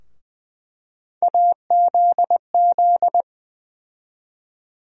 Morse code, heard as a keyed tone, about 20 wpm, 700 Hz. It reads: AZZ
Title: USA Call Sign Postfixes - 3 Letters 20wpm